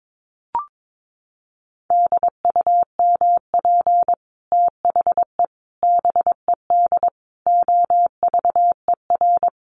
Morse code, heard as a keyed tone, about 22 wpm, 700 Hz. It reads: DUMP THE BED OVER
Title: Sentences from Top 1300 Words 22wpm